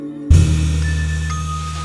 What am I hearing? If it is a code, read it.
There is no code.